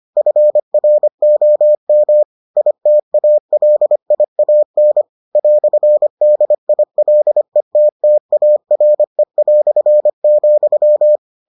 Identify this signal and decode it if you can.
Morse code, heard as a keyed tone, about 25 wpm, 600 Hz. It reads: FROM ITALIAN "DILETTARE",